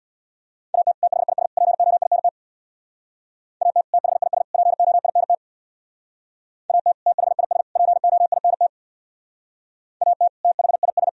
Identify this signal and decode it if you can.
Morse code, heard as a keyed tone, about 63 wpm, 700 Hz. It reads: AT THIS POINT AT THIS POINT AT THIS POINT AT THIS